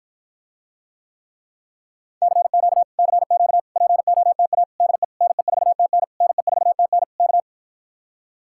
Morse code, heard as a keyed tone, about 53 wpm, 700 Hz. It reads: CQ CQ POTA DE NE4TN NE4TN K